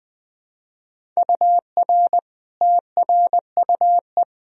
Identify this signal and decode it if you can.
Morse code, heard as a keyed tone, about 20 wpm, 700 Hz. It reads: UR TRUE
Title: Sentences from Top 400 Words and Abbreviations 20wpm